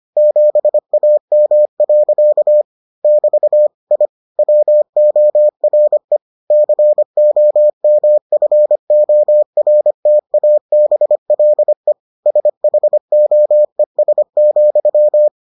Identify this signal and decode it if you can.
Morse code, heard as a keyed tone, about 25 wpm, 600 Hz. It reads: 7AM. <BT> I WORE COMFORTABLE SHOES,